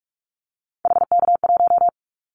Morse code, heard as a keyed tone, about 46 wpm, 700 Hz. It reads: HK1
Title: Callsign Prefixes for Top 100 DXCC - 45wpm